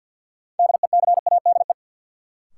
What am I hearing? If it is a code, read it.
Morse code, heard as a keyed tone, about 52 wpm, 700 Hz. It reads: DECADE
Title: Sets of 5 Words - New Word Drill - 50wpm